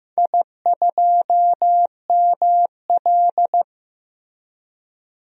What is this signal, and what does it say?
Morse code, heard as a keyed tone, about 15 wpm, 700 Hz. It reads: I2ML
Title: DXCC-100 Call Signs - 4 Characters - Repeated 2x - 15wpm